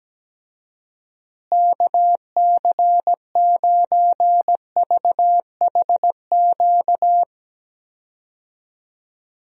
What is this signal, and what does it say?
Morse code, heard as a keyed tone, about 17 wpm, 700 Hz. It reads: KC9VHQ
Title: USA Call Signs - 6 Characters - Repeated 2x - 17wpm